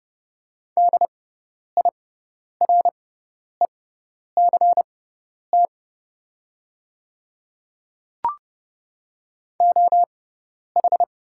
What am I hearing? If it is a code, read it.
Morse code, heard as a keyed tone, about 30 wpm, spaced as for 10 wpm, 700 Hz. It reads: DIRECT OH
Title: 6x Character Spacing - Top 500 Words - 30wpm